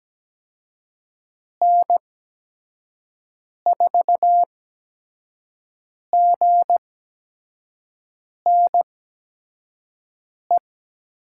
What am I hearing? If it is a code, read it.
Morse code, heard as a keyed tone, about 17 wpm, spaced as for 5 wpm, 700 Hz. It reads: N4GNE
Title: USA Call Signs - 6 Characters - 8x Character Spacing - 17wpm